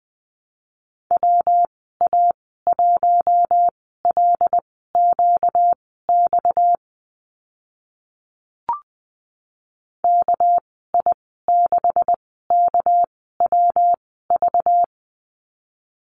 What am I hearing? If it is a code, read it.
Morse code, heard as a keyed tone, about 20 wpm, spaced as for 14 wpm, 700 Hz. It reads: WA1LQX KI6KWV
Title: 245 Lesson 39g - Letter z - Call Signs 20wpm